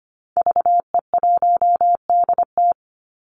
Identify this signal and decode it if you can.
Morse code, heard as a keyed tone, about 25 wpm, 700 Hz. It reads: VE1DT
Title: Call Signs Repeated 3x - 25wpm